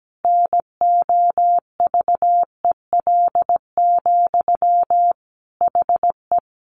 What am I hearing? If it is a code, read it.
Morse code, heard as a keyed tone, about 17 wpm, 700 Hz. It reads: NOVEL, HE
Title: Sentences from Top 1300 Words 17wpm